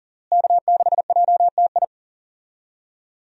Morse code, heard as a keyed tone, about 40 wpm, 700 Hz. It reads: K6JTI
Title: USA Call Signs - 5 Characters - Repeated 3x - 40wpm